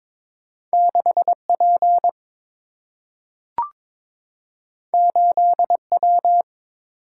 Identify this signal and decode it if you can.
Morse code, heard as a keyed tone, about 22 wpm, 690 Hz. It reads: 6P 8W
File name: practice / 22wpm / Two Letter-Number Combinations - 22wpm